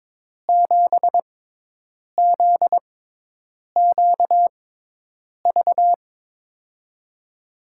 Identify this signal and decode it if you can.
Morse code, heard as a keyed tone, about 22 wpm, spaced as for 8 wpm, 700 Hz. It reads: 7ZQV
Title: USA Call Signs - 6 Characters - 6x Character Spacing - 22wpm